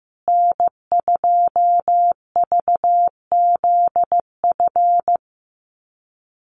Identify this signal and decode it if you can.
Morse code, heard as a keyed tone, about 15 wpm, 700 Hz. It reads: N2VZF
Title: USA Call Signs Repeated 3x 15wpm